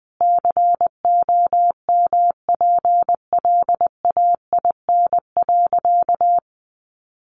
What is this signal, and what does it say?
Morse code, heard as a keyed tone, about 20 wpm, 700 Hz. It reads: COMPLAIN.
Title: Sentences from Top 1300 Words 20wpm